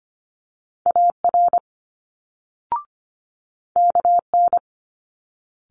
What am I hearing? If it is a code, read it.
Morse code, heard as a keyed tone, about 25 wpm, 700 Hz. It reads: AR KN
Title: Multi-letter Phonemes Quick Fire 25wpm